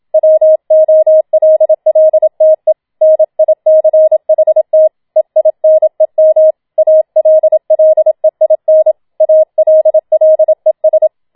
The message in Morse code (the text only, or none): WOLLTE NICHT EINEM ALLEIN ALLES